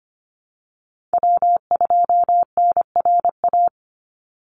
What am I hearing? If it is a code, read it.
Morse code, heard as a keyed tone, about 25 wpm, 700 Hz. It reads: W2NRA
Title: Call Signs Repeated 3x - 25wpm